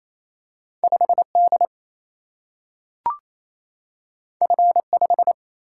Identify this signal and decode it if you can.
Morse code, heard as a keyed tone, about 28 wpm, 700 Hz. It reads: 5D F5
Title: Single-Letter Single-Number 28wpm